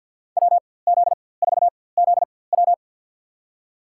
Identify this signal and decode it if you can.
Morse code, heard as a keyed tone, about 52 wpm, spaced as for 24 wpm, 700 Hz. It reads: WZ4ZW
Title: USA Call Signs - 4x Character Spacing - 50wpm